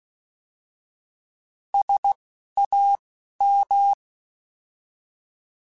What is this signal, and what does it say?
Morse code, heard as a keyed tone, about 16 wpm, spaced as for 11 wpm, 800 Hz. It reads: SAM
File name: Alphabetical / Alphabetical mp3s / Ss @10